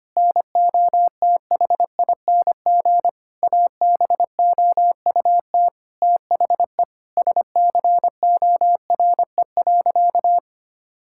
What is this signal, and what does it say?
Morse code, heard as a keyed tone, about 25 wpm, 700 Hz. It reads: NOTHING ABOUT THE SCORE.
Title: Sentences from Top 1600 Words 25wpm